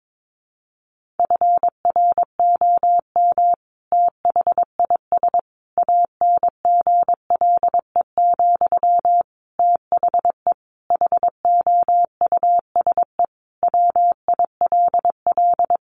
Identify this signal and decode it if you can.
Morse code, heard as a keyed tone, about 22 wpm, 700 Hz. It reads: FROM THIS ANGLE, THE HOUSE WILL
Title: Sentences from Top 1500 Words 22wpm